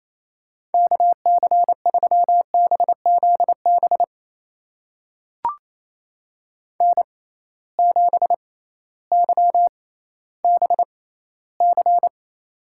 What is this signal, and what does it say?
Morse code, heard as a keyed tone, about 28 wpm, spaced as for 15 wpm, 700 Hz. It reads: KC3BZB N7YBC
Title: USA Call Signs - 6x Character Spacing - 28wpm